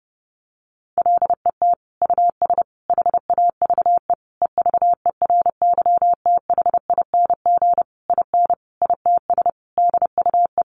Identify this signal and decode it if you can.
Morse code, heard as a keyed tone, about 30 wpm, 700 Hz. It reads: LET US HAVE EVERYTHING IN ITS DUE
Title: Sentences from Top 900 Words 30wpm